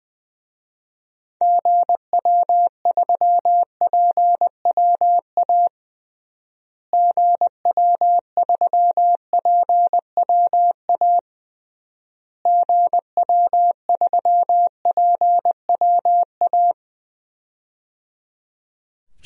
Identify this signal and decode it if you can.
Morse code, heard as a keyed tone, about 20 wpm, 700 Hz. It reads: GW3PWA GW3PWA GW3PWA
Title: DXCC-100 Call Signs - 6 Characters - Repeated 3x - 20wpm